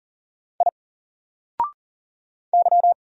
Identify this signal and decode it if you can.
Morse code, heard as a keyed tone, about 40 wpm, 700 Hz. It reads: I Y
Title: Single Character (Letter:Number:Punctuation:Prosign) 40wpm